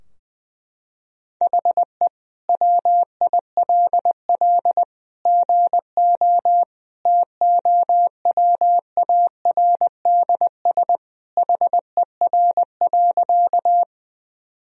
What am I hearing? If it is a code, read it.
Morse code, heard as a keyed tone, about 20 wpm, 700 Hz. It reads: HE WILL GO TOWARDS HER.
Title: Sentences from Top 1000 Words - Review Part 2 - 20wpm